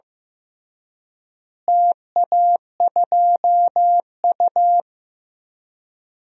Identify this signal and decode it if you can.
Morse code, heard as a keyed tone, about 15 wpm, 700 Hz. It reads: TA2U
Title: DXCC-100 Call Signs - 4 Characters - Repeated 3x - 15wpm